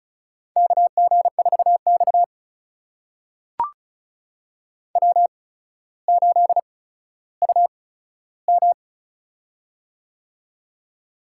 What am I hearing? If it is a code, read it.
Morse code, heard as a keyed tone, about 36 wpm, spaced as for 15 wpm, 700 Hz. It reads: KG4X W8UM
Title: USA Call Signs - 4 Characters - 8x Character Spacing - 35wpm